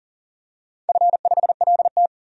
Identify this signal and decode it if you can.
Morse code, heard as a keyed tone, about 40 wpm, 700 Hz. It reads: F5LT
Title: DXCC-100 Call Signs - 4 Characters - 40wpm